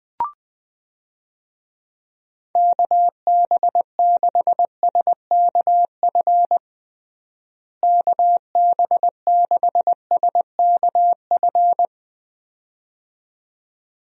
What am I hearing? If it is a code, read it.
Morse code, heard as a keyed tone, about 20 wpm, 700 Hz. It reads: KB6SKF KB6SKF
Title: USA Call Signs - 6 Characters - Repeated 2x - 20wpm